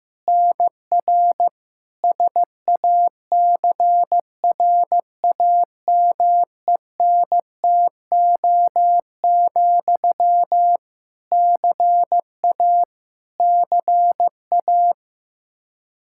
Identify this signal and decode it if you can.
Morse code, heard as a keyed tone, about 15 wpm, 700 Hz. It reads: NR SACRAMENTO, CA CA
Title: K1USN SST QSO Practice 15wpm